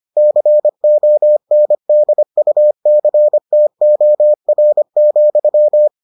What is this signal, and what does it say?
Morse code, heard as a keyed tone, about 25 wpm, 600 Hz. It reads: CONDUCTOR,